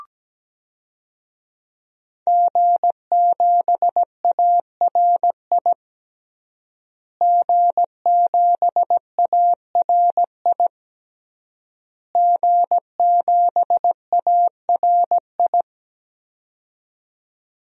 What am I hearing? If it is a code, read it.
Morse code, heard as a keyed tone, about 17 wpm, 700 Hz. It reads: G7ARI G7ARI G7ARI
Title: DXCC-100 Call Signs - 5 Characters - Repeated 3x - 17wpm